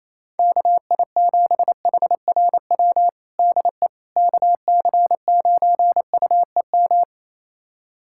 Morse code, heard as a keyed tone, about 28 wpm, 700 Hz. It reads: KI7HRW DE KC9UEM
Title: USA Callsign DE Callsign - 6 Characters - 28wpm